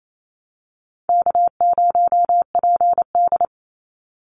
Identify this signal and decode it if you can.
Morse code, heard as a keyed tone, about 28 wpm, 700 Hz. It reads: K0PD
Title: Easy Call Signs Repeated 2x - 28wpm